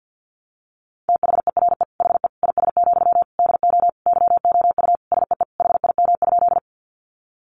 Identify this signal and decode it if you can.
Morse code, heard as a keyed tone, about 50 wpm, 700 Hz. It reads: THERE HE IS, DO YOU SEE HIM?
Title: Sentences from Top 100 Words 50wpm